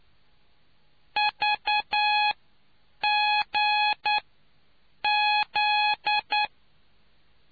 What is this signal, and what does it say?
Morse code, heard as a keyed tone, about 9 wpm, spaced as for 7 wpm, 840 Hz. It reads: VG Z